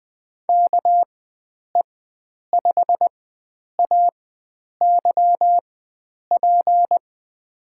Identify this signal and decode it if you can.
Morse code, heard as a keyed tone, about 20 wpm, spaced as for 9 wpm, 700 Hz. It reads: KE5AYP
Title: USA Call Signs - 4x Character Spacing - 20wpm